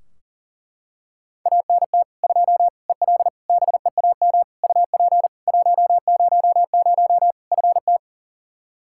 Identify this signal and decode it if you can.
Morse code, heard as a keyed tone, about 40 wpm, 700 Hz. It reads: ANT 2 EL BEAM UP 100 FT
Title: K1USN SST QSO Practice 40wpm